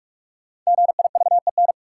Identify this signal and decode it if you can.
Morse code, heard as a keyed tone, about 44 wpm, 700 Hz. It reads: GIVEN
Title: Sentences from Top 1400 Words - New Word Drill Alt - 45wpm